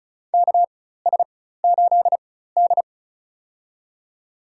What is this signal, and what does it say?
Morse code, heard as a keyed tone, about 36 wpm, spaced as for 16 wpm, 700 Hz. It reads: KS8D
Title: USA Call Signs - 4 Characters - 4x Character Spacing - 35wpm